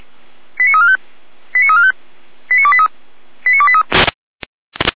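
Five-tone selective call (ZVEI), heard as quick runs of five tones, tones of about 70 ms. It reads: 89345 89345 89283 89283
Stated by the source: ZVEI five-tone selective call 89169 received on SDR, from the feuerwehr-orlamuende.org fire brigade audio archive.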